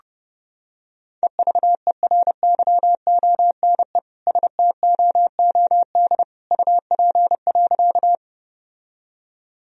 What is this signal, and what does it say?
Morse code, heard as a keyed tone, about 32 wpm, 700 Hz. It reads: EVERYONE STOOD UP.
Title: Sentences from Top 1000 Words - Review Part 2 - 30wpm